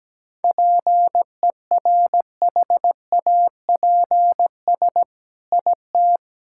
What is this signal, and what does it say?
Morse code, heard as a keyed tone, about 17 wpm, 700 Hz. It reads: PERHAPS IT
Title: Sentences from Top 1600 Words 17wpm